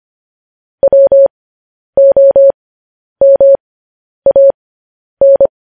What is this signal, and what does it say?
Morse code, heard as a keyed tone, about 25 wpm, spaced as for 10 wpm, 550 Hz. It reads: WOMAN
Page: news headlines in morse code at 10 wpm